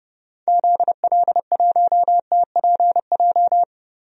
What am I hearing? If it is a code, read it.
Morse code, heard as a keyed tone, about 30 wpm, 700 Hz. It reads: ZL1TPJ
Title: DXCC-100 Call Signs - 6 Characters - Repeated 3x - 30wpm